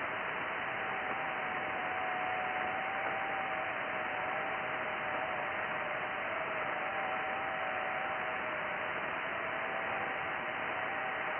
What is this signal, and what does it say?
FT8.
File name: IC-7100